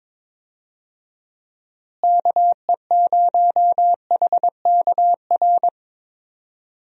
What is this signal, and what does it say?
Morse code, heard as a keyed tone, about 22 wpm, 700 Hz. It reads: KE0HKR
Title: USA Call Signs - 6 Characters - Repeated 3x - 22wpm